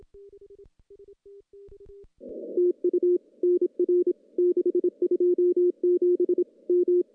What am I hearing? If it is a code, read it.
Morse code, heard as a keyed tone, about 25 wpm, 340 Hz. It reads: TU NR 627 M
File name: contests / CW-SS-k5tr- -0800